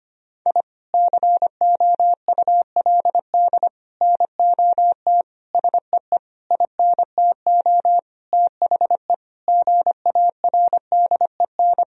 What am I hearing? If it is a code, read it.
Morse code, heard as a keyed tone, about 25 wpm, 700 Hz. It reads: I COULD NOT SEE INTO THE GARDEN
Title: Sentences from Top 1000 Words - Review Part 4 - 25wpm